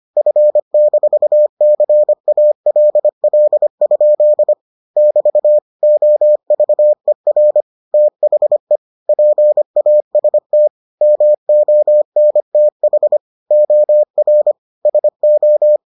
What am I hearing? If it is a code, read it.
Morse code, heard as a keyed tone, about 25 wpm, 600 Hz. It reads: F-CALL? <BT> OVER THE PAST MONTH OR SO